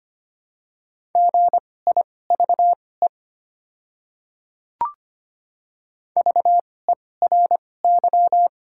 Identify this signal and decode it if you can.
Morse code, heard as a keyed tone, about 25 wpm, spaced as for 18 wpm, 700 Hz. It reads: GIVE VERY
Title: 2x Character Spacing - Top 200 Words - 25wpm